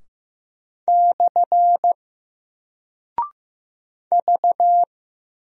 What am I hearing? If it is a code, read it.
Morse code, heard as a keyed tone, about 15 wpm, 700 Hz. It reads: / V
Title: Single Character (Letter:Number:Punctuation:Prosign) Alt 15wpm